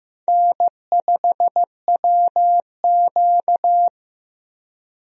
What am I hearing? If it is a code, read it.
Morse code, heard as a keyed tone, about 15 wpm, 700 Hz. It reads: N5WQ
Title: USA Call Signs - 4 Characters - Repeated 2x - 15wpm